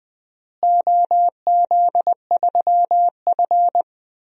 Morse code, heard as a keyed tone, about 20 wpm, 700 Hz. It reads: OZ3F
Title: DXCC-100 Call Signs - 4 Characters - Repeated 3x - 20wpm